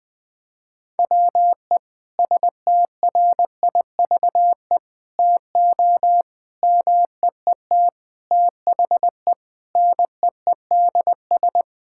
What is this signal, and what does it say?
Morse code, heard as a keyed tone, about 20 wpm, 700 Hz. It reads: WE STRIVE TO MEET THE NEEDS
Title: Sentences with 5-Word Phrases - Spoken Beginning - 20wpm